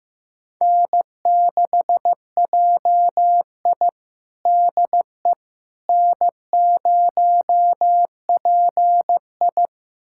Morse code, heard as a keyed tone, about 15 wpm, 700 Hz. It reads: N6JI DE N0PI
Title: USA Callsign DE Callsign - 4 Characters - 15wpm